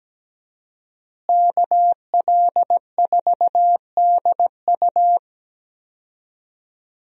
Morse code, heard as a keyed tone, about 17 wpm, 700 Hz. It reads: KL4DU
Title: USA Call Signs - 5 Characters - Repeated 3x - 17wpm